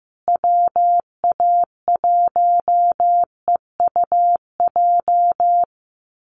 Morse code, heard as a keyed tone, about 15 wpm, 700 Hz. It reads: WA1EUJ
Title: USA Call Signs Repeated 3x 15wpm